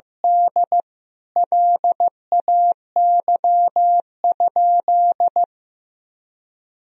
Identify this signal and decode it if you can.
Morse code, heard as a keyed tone, about 16 wpm, 700 Hz. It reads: D LAY?
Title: Sentences from Top 400 Words and Abbreviations 15wpm